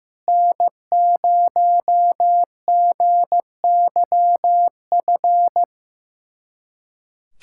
Morse code, heard as a keyed tone, about 15 wpm, 700 Hz. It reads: N0GYF
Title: USA Callsign DE Callsign - 5 Characters - 15wpm